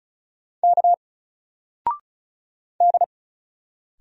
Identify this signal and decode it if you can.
Morse code, heard as a keyed tone, about 35 wpm, 700 Hz. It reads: K D
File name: Single-Letter-Number-35wpm